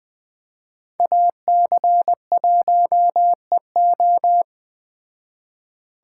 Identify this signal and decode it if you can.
Morse code, heard as a keyed tone, about 21 wpm, 700 Hz. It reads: AC1EO
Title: Speed Racing - USA 2x2 Call Signs 17wpm